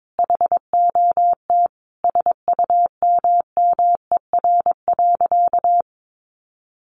Morse code, heard as a keyed tone, about 22 wpm, 700 Hz. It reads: HOT SUMMER.
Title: Sentences from Top 1100 Words 22wpm